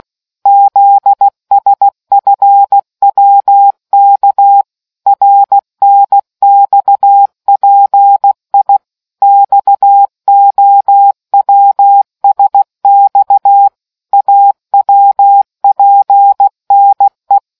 Morse code, heard as a keyed tone, about 16 wpm, 800 Hz. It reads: ZSFWK RNXPI XOWSX AWPNE